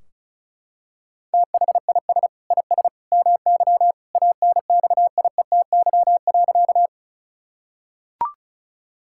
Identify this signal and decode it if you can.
Morse code, heard as a keyed tone, about 34 wpm, 700 Hz. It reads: THIS IS MY ANXIETY.
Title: Sentences from Top 1600 Words 35wpm